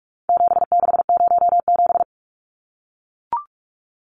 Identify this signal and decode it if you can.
Morse code, heard as a keyed tone, about 44 wpm, 700 Hz. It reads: 7607
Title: Numbers - 4-Digit 45wpm